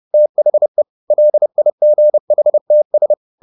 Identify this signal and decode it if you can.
Morse code, heard as a keyed tone, about 30 wpm, 600 Hz. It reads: THE LIGHTS